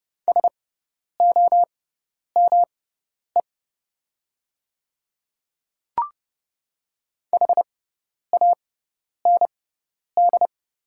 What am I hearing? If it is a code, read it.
Morse code, heard as a keyed tone, about 29 wpm, spaced as for 10 wpm, 690 Hz. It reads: SOME HAND
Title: 6x Character Spacing - Top 200 Words - 30wpm